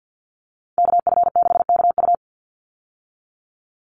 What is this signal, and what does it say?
Morse code, heard as a keyed tone, about 49 wpm, 700 Hz. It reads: KF6KU
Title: USA Call Signs 50wpm